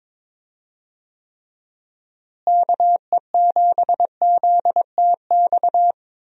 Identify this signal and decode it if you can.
Morse code, heard as a keyed tone, about 22 wpm, 700 Hz. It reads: KE7ZTX
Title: USA Call Signs - 6 Characters - Repeated 3x - 22wpm